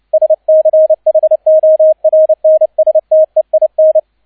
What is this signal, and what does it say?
Morse code, heard as a keyed tone, about 29 wpm, 630 Hz, with radio noise. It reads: SCHORNSTEIN
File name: 002 Zweiter Streich 30wpm